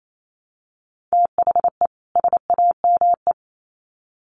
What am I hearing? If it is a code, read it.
Morse code, heard as a keyed tone, about 28 wpm, 700 Hz. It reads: THE SAME
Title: Sets of 2 Words 28wpm